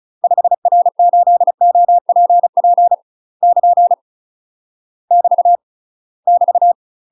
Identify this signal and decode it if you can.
Morse code, heard as a keyed tone, about 34 wpm, 700 Hz. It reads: 5R8OPP <KN> <BT> <BT>
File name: UK MASTER QSOs TRAINING/35WPM